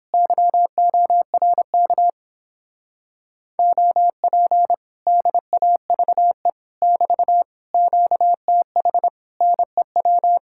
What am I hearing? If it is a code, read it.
Morse code, heard as a keyed tone, about 29 wpm, 700 Hz. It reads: YORK OP DAVE <BT> QTH NEW